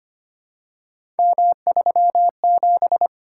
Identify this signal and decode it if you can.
Morse code, heard as a keyed tone, about 25 wpm, 700 Hz. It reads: M37
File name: Three Letter-Number Combinations - 25wpm